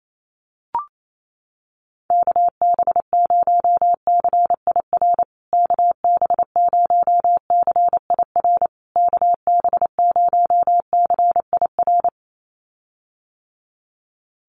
Morse code, heard as a keyed tone, about 28 wpm, 700 Hz. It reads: KB0CIR KB0CIR KB0CIR
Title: USA Call Signs Repeated 3x 28wpm